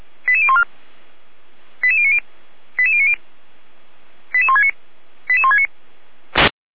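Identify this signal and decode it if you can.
Five-tone selective call (ZVEI), heard as quick runs of five tones, tones of about 70 ms. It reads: 88014 88090 88090 88169 88169
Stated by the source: ZVEI five-tone selective call received on SDR.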